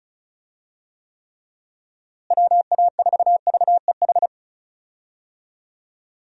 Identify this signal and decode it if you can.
Morse code, heard as a keyed tone, about 34 wpm, 700 Hz. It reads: WA4VEH
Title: USA Call Signs - 6 Characters - Repeated 2x - 35wpm